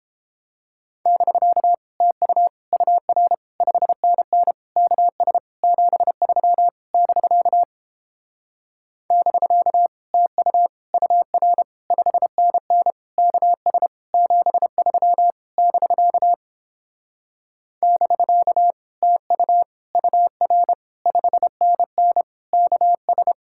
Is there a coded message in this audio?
Yes: Morse code from 1.1 to 23.3 s, heard as a keyed tone, about 31 wpm, 700 Hz. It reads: <BK> TU UR 5NN KS 73 <BK> <BK> TU UR 5NN KS 73 <BK> <BK> TU UR 5NN KS